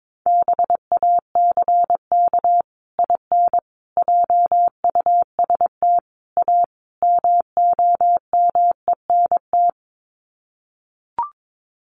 Morse code, heard as a keyed tone, about 22 wpm, 700 Hz. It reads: BACK IN JUST A MOMENT
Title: Sets of 5 Words Encore 22wpm